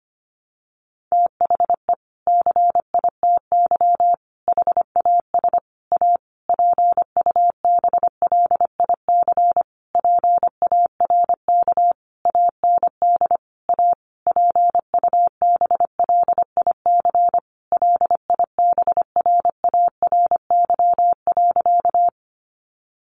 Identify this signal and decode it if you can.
Morse code, heard as a keyed tone, about 25 wpm, 700 Hz. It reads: THE CITY HAS A PUBLIC PARK AND A PUBLIC LIBRARY.
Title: Sentences from Top 1300 Words 25wpm